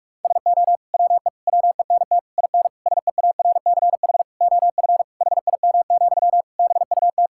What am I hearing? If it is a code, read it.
Morse code, heard as a keyed tone, about 42 wpm, 700 Hz. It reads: SO WE WENT IN SEARCH OF HIM, BUT
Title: Sentences from Top 1000 Words - Review Part 5 - 45wpm